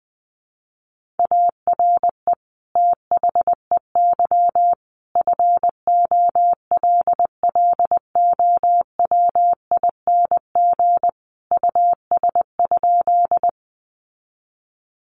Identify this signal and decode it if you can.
Morse code, heard as a keyed tone, about 20 wpm, 700 Hz. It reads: ARE THEY FOLLOWING US?
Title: Sentences from Top 700 Words 20wpm